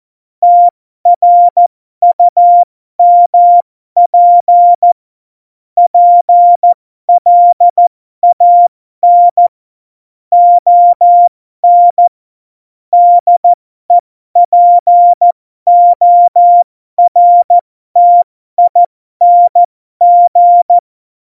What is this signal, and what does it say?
Morse code, heard as a keyed tone, about 14 wpm, spaced as for 12 wpm, 700 Hz. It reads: TRUMP PLAN ON DEPORTING
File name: Morsekurs / Witze